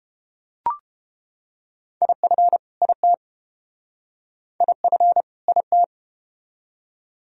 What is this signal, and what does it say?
Morse code, heard as a keyed tone, about 30 wpm, 700 Hz. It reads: IF IT IF IT